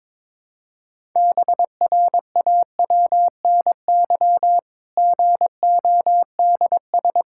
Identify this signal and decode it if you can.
Morse code, heard as a keyed tone, about 22 wpm, 700 Hz. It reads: BRAWNY GODS
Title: Pangram Challenge v2 - 22wpm